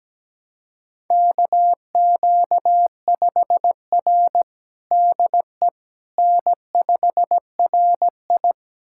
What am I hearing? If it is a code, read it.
Morse code, heard as a keyed tone, about 17 wpm, 700 Hz. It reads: KQ5R DE N5RI
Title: USA Callsign DE Callsign - 4 Characters - 17wpm